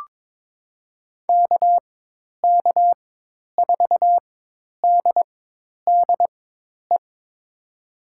Morse code, heard as a keyed tone, about 22 wpm, spaced as for 10 wpm, 700 Hz. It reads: KK4DDE
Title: USA Call Signs - 6 Characters - 4x Character Spacing - 22wpm